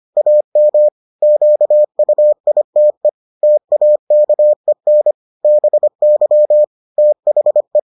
Morse code, heard as a keyed tone, about 25 wpm, 600 Hz. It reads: AM QUITE TAKEN BY THE